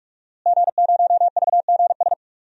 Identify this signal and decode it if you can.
Morse code, heard as a keyed tone, about 46 wpm, 700 Hz. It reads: G0VGS
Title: DXCC-100 Call Signs - 5 Characters - Repeated 3x - 45wpm